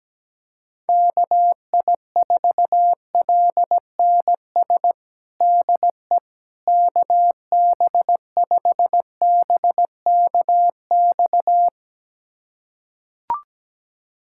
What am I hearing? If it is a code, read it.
Morse code, heard as a keyed tone, about 17 wpm, 700 Hz. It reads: KI4LNS DE KB5BKX
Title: USA Callsign DE Callsign 17wpm